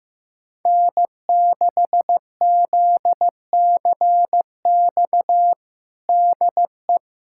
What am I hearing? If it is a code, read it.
Morse code, heard as a keyed tone, about 15 wpm, 700 Hz. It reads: N6ZCX DE
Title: USA Callsign DE Callsign 15wpm